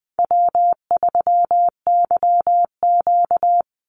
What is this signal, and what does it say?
Morse code, heard as a keyed tone, about 20 wpm, 700 Hz. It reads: W3YQ
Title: Call Signs Repeated 2x - 20wpm